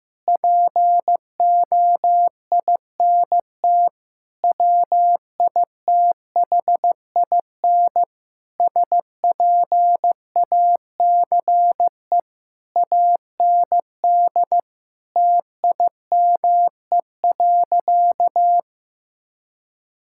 Morse code, heard as a keyed tone, about 15 wpm, 700 Hz. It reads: POINT WITHIN SPACE AND TIME.
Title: Sentences from Top 1000 Words - Review Part 5 - 15wpm